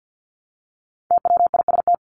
Morse code, heard as a keyed tone, about 50 wpm, 700 Hz. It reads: TWIST
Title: Top 1751-2000 Words 50wpm